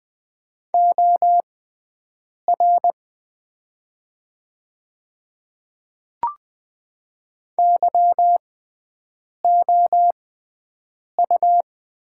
Morse code, heard as a keyed tone, about 20 wpm, spaced as for 7 wpm, 700 Hz. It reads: OR YOU
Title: 6x Character Spacing - Top 200 Words - 20wpm